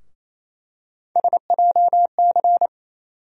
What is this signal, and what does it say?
Morse code, heard as a keyed tone, about 28 wpm, 700 Hz. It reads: SJC
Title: USA Call Sign Postfixes - 3 Letters 28wpm